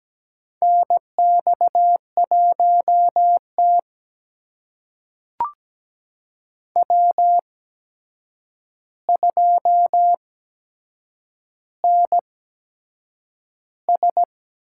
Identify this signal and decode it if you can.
Morse code, heard as a keyed tone, about 17 wpm, spaced as for 7 wpm, 700 Hz. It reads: NX1T W2NS
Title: USA Call Signs - 4 Characters - 8x Character Spacing - 17wpm